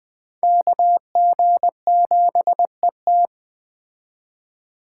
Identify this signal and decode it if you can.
Morse code, heard as a keyed tone, about 20 wpm, 700 Hz. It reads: KG7ET